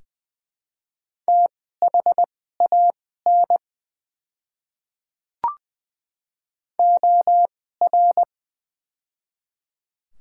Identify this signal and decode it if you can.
Morse code, heard as a keyed tone, about 21 wpm, spaced as for 15 wpm, 700 Hz. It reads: THAN OR
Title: 2x Character Spacing - Top 300 Words - 20wpm